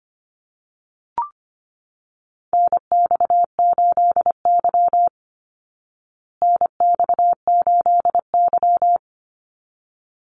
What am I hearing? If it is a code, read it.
Morse code, heard as a keyed tone, about 25 wpm, 700 Hz. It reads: NX8Y NX8Y
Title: Call Signs Repeated 3x - 25wpm